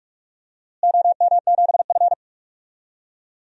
Morse code, heard as a keyed tone, about 46 wpm, 700 Hz. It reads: OM7F
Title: DXCC-100 Call Signs - 4 Characters - Repeated 2x - 45wpm